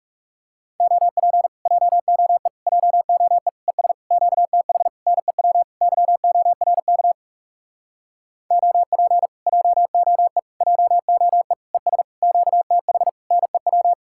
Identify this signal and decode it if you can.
Morse code, heard as a keyed tone, about 40 wpm, 700 Hz. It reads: OP JOE JOE ES QTH NEW YORK OP JOE JOE ES QTH NEW